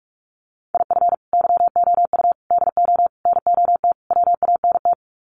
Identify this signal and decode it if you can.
Morse code, heard as a keyed tone, about 44 wpm, 700 Hz. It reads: IF YOU DO NOT WANT
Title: Sets of 5 Words 45wpm